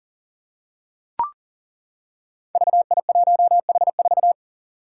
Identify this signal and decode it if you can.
Morse code, heard as a keyed tone, about 40 wpm, 700 Hz. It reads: VI1H4
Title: Five Challenging Letter-Number Combinations 40wpm